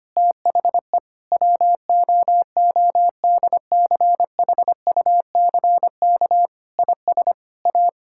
Morse code, heard as a keyed tone, about 25 wpm, 700 Hz. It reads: THE WOODCHUCK IS A
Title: Pangram Challenge v2 - 25wpm